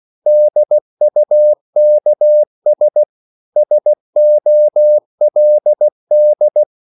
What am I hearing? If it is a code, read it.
Morse code, heard as a keyed tone, about 16 wpm, 600 Hz. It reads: DUKS SOLD
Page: Korte danske ord 16wpm